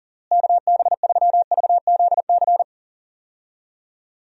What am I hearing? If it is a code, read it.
Morse code, heard as a keyed tone, about 40 wpm, 700 Hz. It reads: KB3VZC
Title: USA Call Signs - 6 Characters - Repeated 3x - 40wpm